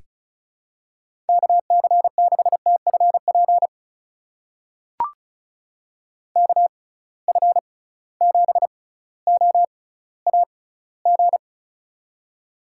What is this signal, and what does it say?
Morse code, heard as a keyed tone, about 36 wpm, spaced as for 18 wpm, 700 Hz. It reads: KC6TFP KF7OAG
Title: USA Call Signs - 6 Characters - 6x Character Spacing - 35wpm